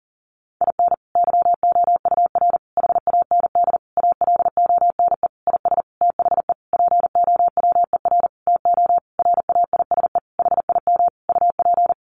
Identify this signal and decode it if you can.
Morse code, heard as a keyed tone, about 40 wpm, 700 Hz. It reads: IN YOUR HAND ALONE IS THE POWER TO RAISE HIM UP
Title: Sentences from Top 1400 Words 40wpm